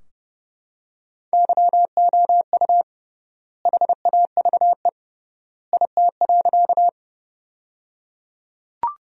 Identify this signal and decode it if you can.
Morse code, heard as a keyed tone, about 30 wpm, 700 Hz. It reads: YOU HAVE IT.
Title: 3x Word-Spacing - Sentences from Top 100 Words - 30wpm